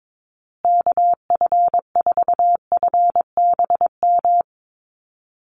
Morse code, heard as a keyed tone, about 22 wpm, 700 Hz. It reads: KF4FBM
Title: USA Call Signs Repeated 2x 22wpm